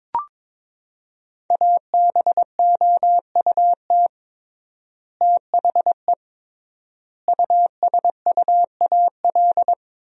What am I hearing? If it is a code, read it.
Morse code, heard as a keyed tone, about 22 wpm, 700 Hz. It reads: ABOUT THE USUAL
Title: 3x Word-Spacing - Sentences from Top 400 Words - 22wpm